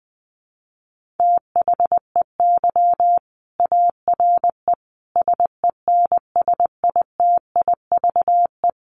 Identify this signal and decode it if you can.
Morse code, heard as a keyed tone, about 19 wpm, 700 Hz. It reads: THEY ARE SENSITIVE
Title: Sentences from Top 1400 Words 20wpm